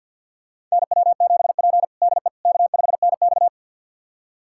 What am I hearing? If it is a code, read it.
Morse code, heard as a keyed tone, about 52 wpm, 700 Hz. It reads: NW7P DE K5NX
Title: USA Callsign DE Callsign - 4 Characters - 50wpm